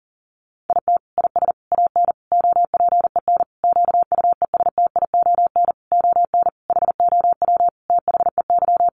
Morse code, heard as a keyed tone, about 40 wpm, 700 Hz. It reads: IT IS AN OPEN QUESTION ON HOW THEY
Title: Sentences from Top 1300 Words 40wpm